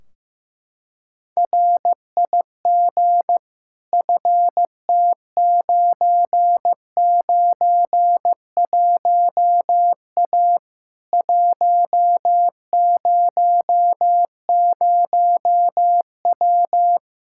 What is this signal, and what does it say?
Morse code, heard as a keyed tone, about 15 wpm, 700 Hz. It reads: RIG FT991A 100W
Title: Practice QSO - Exchange 2 - Info - 15wpm